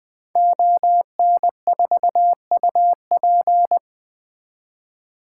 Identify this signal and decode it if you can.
Morse code, heard as a keyed tone, about 20 wpm, 700 Hz. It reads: ON4UP
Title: 2023 SOTA Activator Call Signs - 20wpm